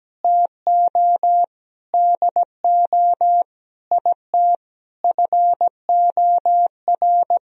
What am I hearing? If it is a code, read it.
Morse code, heard as a keyed tone, about 17 wpm, 700 Hz. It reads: TO DO IT FOR
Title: Sentences from Top 1000 Words - Review Part 3 - 17wpm